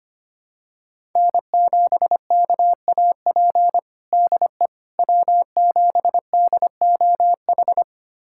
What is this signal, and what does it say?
Morse code, heard as a keyed tone, about 25 wpm, 700 Hz. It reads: N7KAP DE W7DOH
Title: USA Callsign DE Callsign - 5 Characters - 25wpm